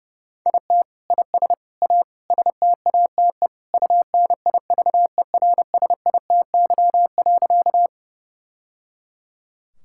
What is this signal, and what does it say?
Morse code, heard as a keyed tone, about 30 wpm, 700 Hz. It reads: IT IS A STATE UNIVERSITY.
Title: Sentences from Top 1000 Words - Review Part 5 - 30wpm